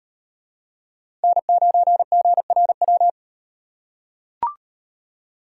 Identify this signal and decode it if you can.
Morse code, heard as a keyed tone, about 38 wpm, 700 Hz. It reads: N9GRW